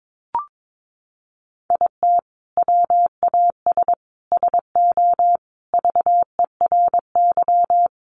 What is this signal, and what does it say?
Morse code, heard as a keyed tone, about 22 wpm, 700 Hz. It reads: IT WAS SO VERY
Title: Sentences from Top 500 Words 22wpm